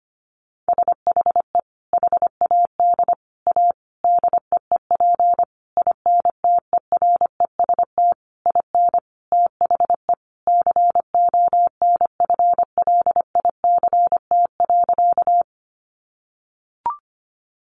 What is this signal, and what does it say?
Morse code, heard as a keyed tone, about 25 wpm, 700 Hz. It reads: SHE HAD A DEEP INTEREST IN THE CONFLICT.
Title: Sentences from Top 1300 Words 25wpm